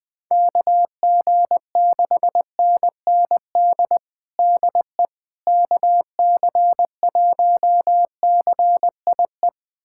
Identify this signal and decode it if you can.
Morse code, heard as a keyed tone, about 21 wpm, 700 Hz. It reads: KG6NND DE KC1CIE
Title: USA Callsign DE Callsign - 6 Characters - 20wpm